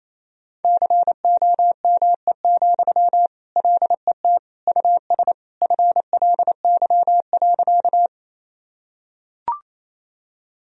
Morse code, heard as a keyed tone, about 28 wpm, 700 Hz. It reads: COME, LET US FLY.
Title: Sentences from Top 1000 Words - Review Part 2 - 28wpm